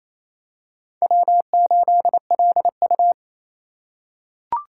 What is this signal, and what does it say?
Morse code, heard as a keyed tone, about 28 wpm, 700 Hz. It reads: W8LU
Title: USA Call Signs - 4 Characters - 2x Character Spacing - 28wpm